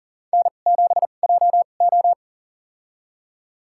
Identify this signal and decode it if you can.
Morse code, heard as a keyed tone, about 40 wpm, spaced as for 29 wpm, 700 Hz. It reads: N7JO
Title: USA Call Signs - 4 Characters - 2x Character Spacing - 40wpm